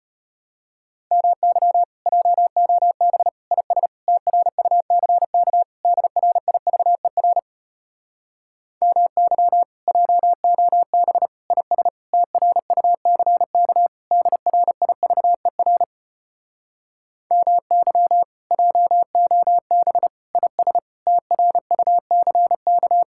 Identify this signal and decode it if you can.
Morse code, heard as a keyed tone, about 33 wpm, 700 Hz. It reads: MY JOB IS TRUCK DRIVER MY JOB IS TRUCK DRIVER MY JOB IS TRUCK